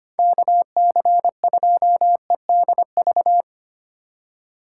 Morse code, heard as a keyed tone, about 25 wpm, 700 Hz. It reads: KC2EDV
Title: USA Call Signs - 6 Characters - 2x Character Spacing - 25wpm